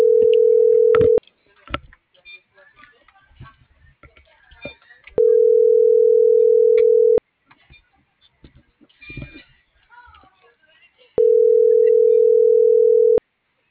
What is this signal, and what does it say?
Ringback tone (North America), 440 + 480 Hz.